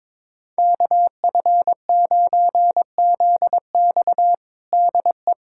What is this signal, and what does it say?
Morse code, heard as a keyed tone, about 21 wpm, 700 Hz. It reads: KF9ZX DE
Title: USA Callsign DE Callsign - 5 Characters - 22wpm